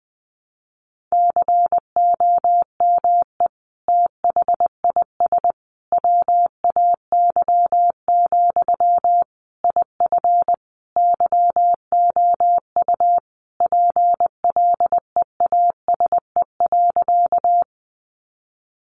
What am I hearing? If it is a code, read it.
Morse code, heard as a keyed tone, about 20 wpm, 700 Hz. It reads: COME THIS WAY, IF YOU PLEASE.
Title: Sentences from Top 1500 Words 20wpm